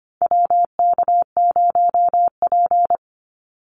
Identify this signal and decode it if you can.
Morse code, heard as a keyed tone, about 24 wpm, 700 Hz. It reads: WK0P
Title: Call Signs - 25wpm